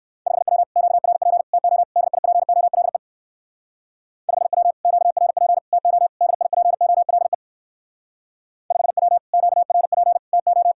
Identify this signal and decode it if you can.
Morse code, heard as a keyed tone, about 57 wpm, 690 Hz. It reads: 5W QRP TO DIPOLE 5W QRP TO DIPOLE 5W QRP TO